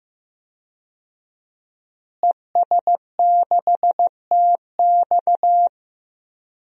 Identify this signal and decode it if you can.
Morse code, heard as a keyed tone, about 16 wpm, 700 Hz. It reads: ES6TX
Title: DXCC-100 Call Signs - 5 Characters - Repeated 2x - 15wpm